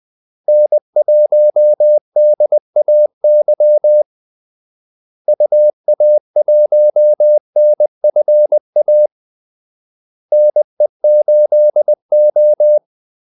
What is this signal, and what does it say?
Morse code, heard as a keyed tone, about 20 wpm, 600 Hz. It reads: N1DAY UA1NFA NE8O
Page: Kaldesignaler 20 wpm